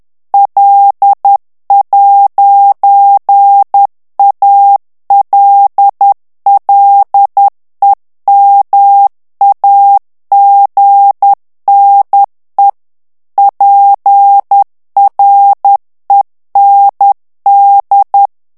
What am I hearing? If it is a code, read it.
Morse code, heard as a keyed tone, about 10 wpm, 800 Hz. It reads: L'ALLEMAGNE PREND
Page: Français - texte en clair 1 - vitesse : 10 mots minute